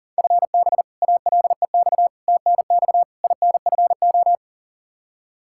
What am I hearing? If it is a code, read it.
Morse code, heard as a keyed tone, about 37 wpm, 700 Hz. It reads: FB ALEX TNX INFO
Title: Practice QSO - Exchange 3 - More Info - 40wpm